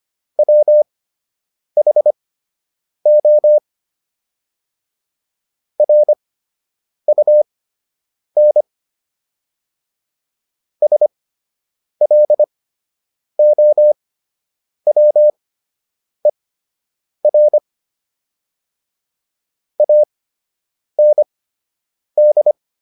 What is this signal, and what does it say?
Morse code, heard as a keyed tone, about 25 wpm, spaced as for 8 wpm, 600 Hz. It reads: WHO RUN SLOWER AND